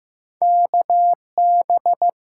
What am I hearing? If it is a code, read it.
Morse code, heard as a keyed tone, about 15 wpm, 700 Hz. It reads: KB
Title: USA Call Signs - 8x Character Spacing - 15wpm